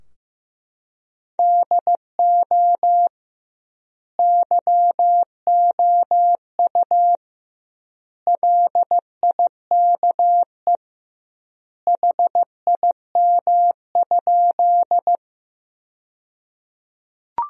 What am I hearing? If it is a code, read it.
Morse code, heard as a keyed tone, about 15 wpm, 700 Hz. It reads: DO YOU LIKE HIM?
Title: 2x Word-Spacing - Sentences from Top 100 Words - 15wpm